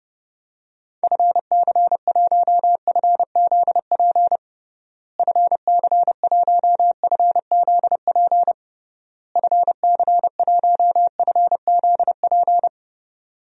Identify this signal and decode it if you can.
Morse code, heard as a keyed tone, about 30 wpm, 700 Hz. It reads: FC1FZP FC1FZP FC1FZP
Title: DXCC-100 Call Signs - 6 Characters - Repeated 3x - 30wpm